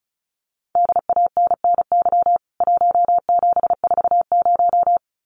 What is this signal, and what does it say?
Morse code, heard as a keyed tone, about 34 wpm, 700 Hz. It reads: DANNY 1740
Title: CWT Exchange 35wpm